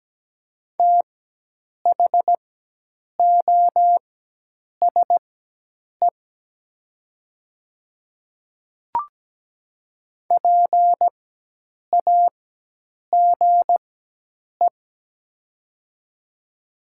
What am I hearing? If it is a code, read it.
Morse code, heard as a keyed tone, about 17 wpm, spaced as for 8 wpm, 700 Hz. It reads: THOSE PAGE
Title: 4x Character Spacing - Top 300 Words - 17wpm